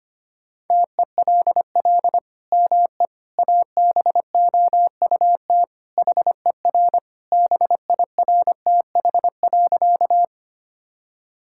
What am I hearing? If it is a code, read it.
Morse code, heard as a keyed tone, about 25 wpm, 700 Hz. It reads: TELL ME ABOUT HER BIRTH.
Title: Sentences from Top 1000 Words - Review Part 5 - 25wpm